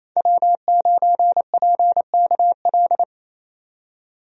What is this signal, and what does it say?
Morse code, heard as a keyed tone, about 27 wpm, 700 Hz. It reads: W9PKL
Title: USA Call Signs - 5 Characters - 28wpm